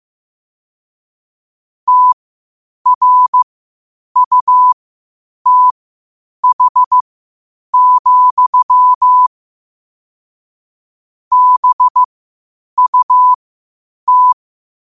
Morse code, heard as a keyed tone, about 15 wpm, spaced as for 8 wpm, 1000 Hz. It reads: TRUTH, BUT
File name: qotdfiles / 8wpm